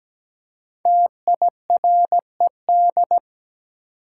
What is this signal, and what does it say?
Morse code, heard as a keyed tone, about 17 wpm, 700 Hz. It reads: TIRED
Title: Five Letter Words - 17wpm